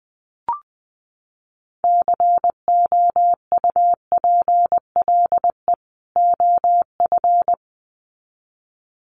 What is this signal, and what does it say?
Morse code, heard as a keyed tone, about 20 wpm, 700 Hz. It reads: COUPLE OF
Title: Sets of 2 Words 20wpm